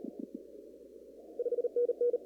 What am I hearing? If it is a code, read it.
Morse code, heard as a keyed tone, about 35 wpm, 480 Hz. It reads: 5NN
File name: CW / kt5j wpx cw-op-k5pi